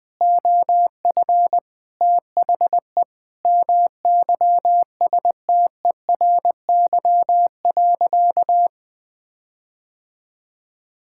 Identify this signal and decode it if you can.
Morse code, heard as a keyed tone, about 20 wpm, 700 Hz. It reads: OF THE MYSTERY.
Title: Sentences with 6-Word Phrases - Spoken Beginning - 20wpm